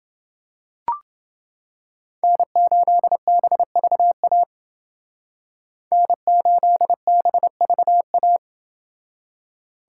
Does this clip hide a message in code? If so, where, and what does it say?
Yes: Morse code from 2.2 to 8.4 s, heard as a keyed tone, about 29 wpm, 700 Hz. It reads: N8BVA N8BVA